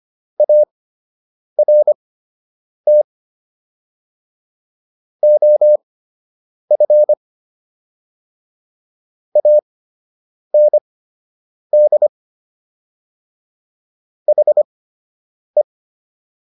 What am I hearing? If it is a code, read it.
Morse code, heard as a keyed tone, about 25 wpm, spaced as for 8 wpm, 600 Hz. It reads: ART OF AND HE